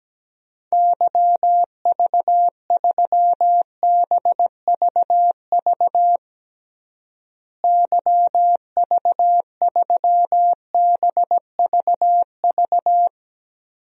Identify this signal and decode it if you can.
Morse code, heard as a keyed tone, about 17 wpm, 700 Hz. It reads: YV3BVV YV3BVV
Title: DXCC-100 Call Signs - 6 Characters - Repeated 3x - 17wpm